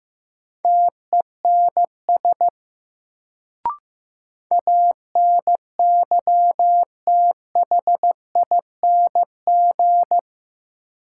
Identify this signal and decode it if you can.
Morse code, heard as a keyed tone, about 15 wpm, 700 Hz. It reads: TENS ANYTHING
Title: Sets of 4 Words - New Word Drill - 15wpm